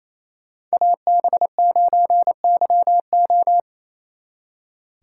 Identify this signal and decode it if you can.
Morse code, heard as a keyed tone, about 34 wpm, 700 Hz. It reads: AB9YO
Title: Speed Racing - USA Call Signs 28wpm